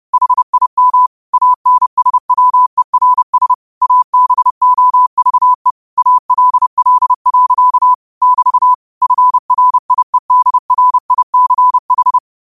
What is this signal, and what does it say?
Morse code, heard as a keyed tone, about 30 wpm, 1000 Hz. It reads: HIM ANSWERS ABOVE ALL. <BT> FRIEDRICH
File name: qotdfiles / 30wpm